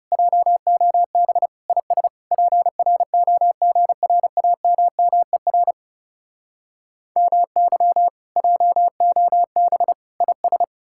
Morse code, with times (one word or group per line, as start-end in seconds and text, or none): 0.1-1.5 JOB
1.7-2.1 IS
2.3-5.7 PROGRAMMER
7.2-8.1 MY
8.4-9.9 JOB
10.2-10.6 IS